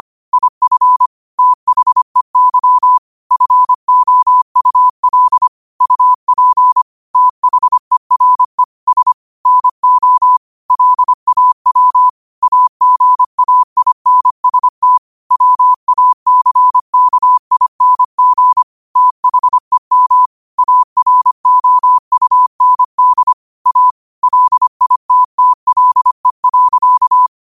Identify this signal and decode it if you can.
Morse code, heard as a keyed tone, about 25 wpm, 1000 Hz. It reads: IF THEY FOUL UP THERE S NO LAW AGAINST WACKING THEM AROUND A LITTLE.